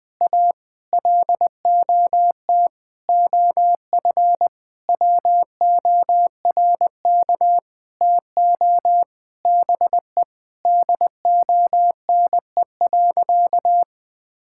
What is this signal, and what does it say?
Morse code, heard as a keyed tone, about 20 wpm, 700 Hz. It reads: A LOT OF WORK TO BE DONE.
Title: Sentences with 9-Word Phrases - Spoken Beginning - 20wpm